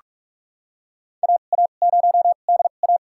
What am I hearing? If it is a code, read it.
Morse code, heard as a keyed tone, about 47 wpm, spaced as for 33 wpm, 700 Hz. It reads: AA0DA
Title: USA Call Signs - 5 Characters - 2x Character Spacing - 45wpm